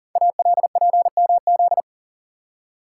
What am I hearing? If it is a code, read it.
Morse code, heard as a keyed tone, about 40 wpm, 700 Hz. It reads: ALPMZ
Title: Code Groups - 5 Letter:Numbers - 40wpm